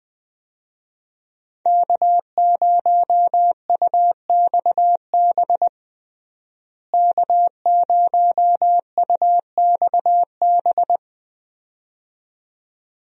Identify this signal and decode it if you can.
Morse code, heard as a keyed tone, about 20 wpm, 700 Hz. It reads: K0UXB K0UXB
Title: USA Call Signs - 5 Characters - Repeated 2x - 20wpm